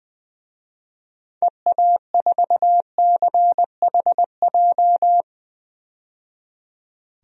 Morse code, heard as a keyed tone, about 20 wpm, 700 Hz. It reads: EA4CHJ
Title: DXCC-100 Call Signs - 6 Characters - 20wpm